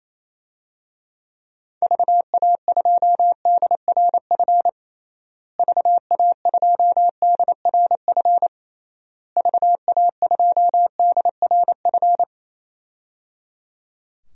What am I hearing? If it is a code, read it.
Morse code, heard as a keyed tone, about 28 wpm, 700 Hz. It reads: VA2DRF VA2DRF VA2DRF
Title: DXCC-100 Call Signs - 6 Characters - Repeated 3x - 28wpm